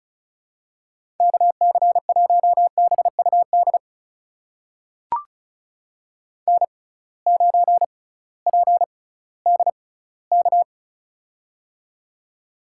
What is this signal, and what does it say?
Morse code, heard as a keyed tone, about 36 wpm, spaced as for 19 wpm, 700 Hz. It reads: KC1BUD N9PDK
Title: USA Call Signs - 6x Character Spacing - 35wpm